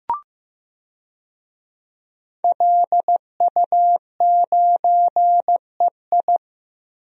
Morse code, heard as a keyed tone, about 15 wpm, 700 Hz. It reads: LU9EI
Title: DXCC-100 Call Signs - 5 Characters - 15wpm